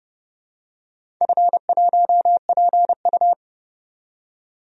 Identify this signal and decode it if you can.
Morse code, heard as a keyed tone, about 30 wpm, 700 Hz. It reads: F1PU
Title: DXCC-100 Call Signs - 4 Characters - 30wpm